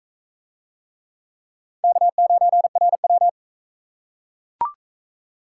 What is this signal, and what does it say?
Morse code, heard as a keyed tone, about 35 wpm, 700 Hz. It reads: K9RW